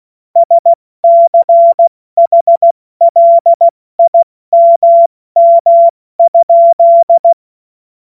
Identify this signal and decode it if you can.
Morse code, heard as a keyed tone, about 16 wpm, 680 Hz. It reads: SCHLIMM?
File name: Morsekurs / Witze / Witz 26 März 2024 70 F80